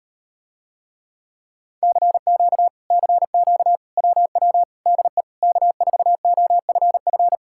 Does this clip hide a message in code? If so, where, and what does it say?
Yes: Morse code from 1.8 to 7.4 s, heard as a keyed tone, about 39 wpm, 700 Hz. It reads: CQ CQ WW DE K4OFF